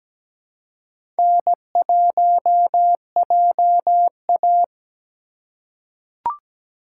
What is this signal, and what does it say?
Morse code, heard as a keyed tone, about 17 wpm, 700 Hz. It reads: N1JA
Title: USA Call Signs - 4 Characters - 2x Character Spacing - 17wpm